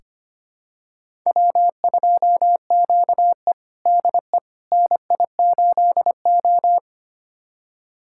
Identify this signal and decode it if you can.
Morse code, heard as a keyed tone, about 25 wpm, 700 Hz. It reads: W2QE DE NI8O
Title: USA Callsign DE Callsign - 4 Characters - 25wpm